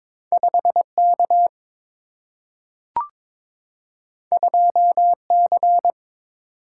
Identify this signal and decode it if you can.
Morse code, heard as a keyed tone, about 22 wpm, 700 Hz. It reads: 5K 2C
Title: Single-Letter Single-Number 22wpm